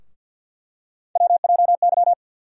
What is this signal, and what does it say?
Morse code, heard as a keyed tone, about 47 wpm, 700 Hz. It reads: WJY
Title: USA Call Sign Postfixes - 3 Letters 50wpm